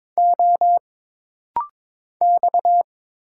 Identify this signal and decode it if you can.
Morse code, heard as a keyed tone, about 22 wpm, 700 Hz. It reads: O X
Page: This rapid-fire timing is geared to push your ICR (Instant Character Recognition) skills.